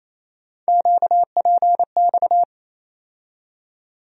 Morse code, heard as a keyed tone, about 28 wpm, 700 Hz. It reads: QPX
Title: USA Call Sign Postfixes - 3 Letters 28wpm